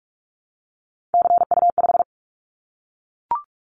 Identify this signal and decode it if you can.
Morse code, heard as a keyed tone, about 44 wpm, 700 Hz. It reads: CU5
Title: Callsign Prefixes for Top 50 DXCC - 45wpm